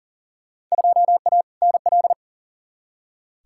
Morse code, heard as a keyed tone, about 40 wpm, 700 Hz. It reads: 2A NL
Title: Field Day Exchanges Encore - 40wpm